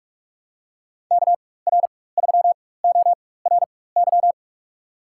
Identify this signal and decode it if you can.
Morse code, heard as a keyed tone, about 46 wpm, spaced as for 21 wpm, 700 Hz. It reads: KR3ORY
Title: USA Call Signs - 4x Character Spacing - 45wpm